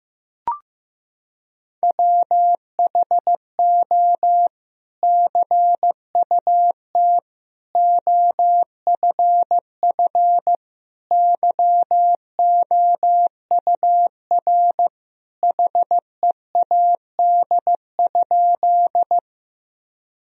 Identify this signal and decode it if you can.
Morse code, heard as a keyed tone, about 16 wpm, 700 Hz. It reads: WHO CUT OFF YOUR HEAD?
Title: Sentences from Top 1000 Words - Review Part 1 - 15wpm